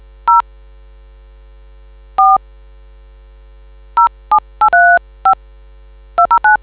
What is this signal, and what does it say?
DTMF, heard as telephone touch tones, tones of about 120 ms. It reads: 0 4 07835 209